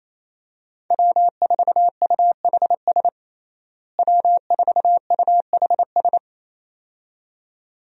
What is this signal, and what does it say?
Morse code, heard as a keyed tone, about 28 wpm, 700 Hz. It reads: W4UHS W4UHS
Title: USA Call Signs - 5 Characters - Repeated 2x - 28wpm